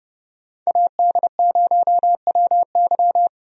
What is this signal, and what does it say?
Morse code, heard as a keyed tone, about 30 wpm, 700 Hz. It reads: AD0WY
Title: USA Call Signs - 5 Characters - Repeated 3x - 30wpm